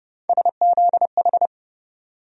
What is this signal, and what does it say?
Morse code, heard as a keyed tone, about 32 wpm, 700 Hz. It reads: SZH
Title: USA Call Sign Postfixes - 3 Letters 30wpm